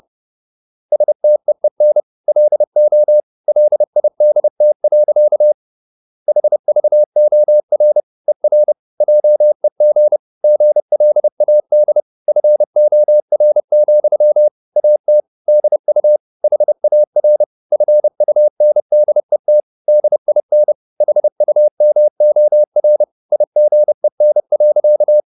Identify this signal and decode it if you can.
Morse code, heard as a keyed tone, about 30 wpm, 600 Hz. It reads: STEEN LO LIDT. HVOR ER JEG GLAD FOR, AT DU HAR FUNDET DIN HUMOR IGEN.
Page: Ferie 30 wpm